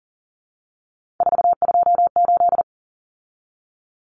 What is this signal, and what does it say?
Morse code, heard as a keyed tone, about 39 wpm, 700 Hz. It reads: 428
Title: Common Cut Numbers - 3-Digit 40wpm